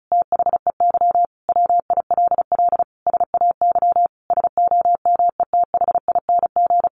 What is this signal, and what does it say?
Morse code, heard as a keyed tone, about 34 wpm, 700 Hz. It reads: THEY WILL SAY SOMETHING
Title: Sentences from Top 1500 Words 35wpm